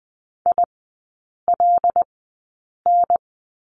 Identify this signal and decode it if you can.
Morse code, heard as a keyed tone, about 20 wpm, spaced as for 8 wpm, 700 Hz. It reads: ILN
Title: 087 Lesson 15e - Number 5 - Three Characters 20wpm